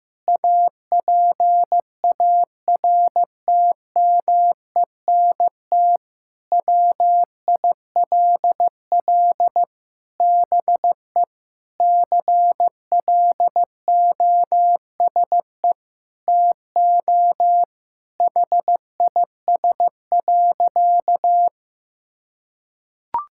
Morse code, heard as a keyed tone, about 15 wpm, 700 Hz. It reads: APARTMENT WILL BE CLOSE TO HIS.
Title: Sentences from Top 1000 Words - Review Part 6 - 15wpm